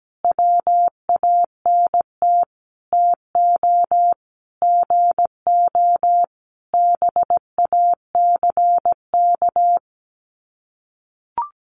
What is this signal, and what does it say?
Morse code, heard as a keyed tone, about 17 wpm, 700 Hz. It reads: WANT TO GO BACK
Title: Sets of 5 Words Encore 17wpm